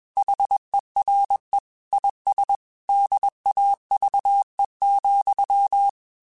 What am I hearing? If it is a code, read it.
Morse code, heard as a keyed tone, about 21 wpm, 800 Hz. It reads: HERE IS DAVE,
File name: QSOs / QSOs-mp3s / WD0BGP 1@20